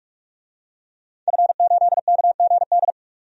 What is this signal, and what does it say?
Morse code, heard as a keyed tone, about 41 wpm, 700 Hz. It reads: F8KGD
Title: DXCC-100 Call Signs - 5 Characters - 45wpm